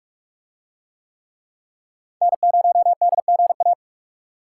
Morse code, heard as a keyed tone, about 44 wpm, 700 Hz. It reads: N0DGA